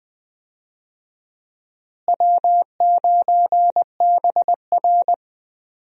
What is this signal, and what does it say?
Morse code, heard as a keyed tone, about 20 wpm, 700 Hz. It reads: W9BR
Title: USA Call Signs - 4 Characters - Repeated 3x - 20wpm